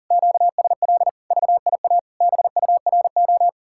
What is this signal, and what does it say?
Morse code, heard as a keyed tone, about 48 wpm, 700 Hz. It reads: QSL VIA BURO
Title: Speed-Racing Common QSO Phrases - 40wpm